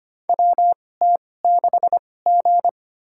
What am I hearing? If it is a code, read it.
Morse code, heard as a keyed tone, about 25 wpm, spaced as for 18 wpm, 700 Hz. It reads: WT6G
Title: USA Call Signs - 4 Characters - 2x Character Spacing - 25wpm